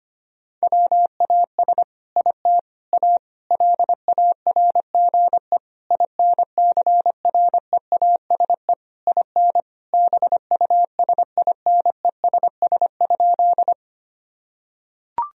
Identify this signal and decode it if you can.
Morse code, heard as a keyed tone, about 25 wpm, 700 Hz. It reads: WAS IT A LARGE INCREASE IN BUSINESS?
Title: Sentences from Top 1000 Words - Review Part 4 - 25wpm